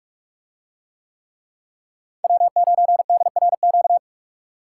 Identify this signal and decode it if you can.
Morse code, heard as a keyed tone, about 46 wpm, 700 Hz. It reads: W9DRQ